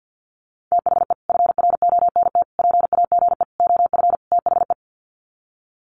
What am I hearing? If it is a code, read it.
Morse code, heard as a keyed tone, about 50 wpm, 700 Hz. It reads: THE FRONT PAGE OF THE
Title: Sets of 5 Words Encore 50wpm